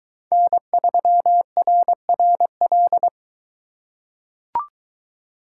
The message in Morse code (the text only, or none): N3RRL